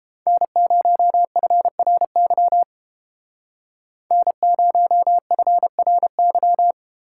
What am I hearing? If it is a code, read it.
Morse code, heard as a keyed tone, about 32 wpm, 700 Hz. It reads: N0FRY N0FRY